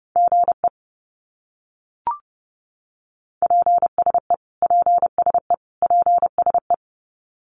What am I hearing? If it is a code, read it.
Morse code, heard as a keyed tone, about 30 wpm, 700 Hz. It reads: GE PSE PSE PSE
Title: Abbreviations Repeated 3x - 30wpm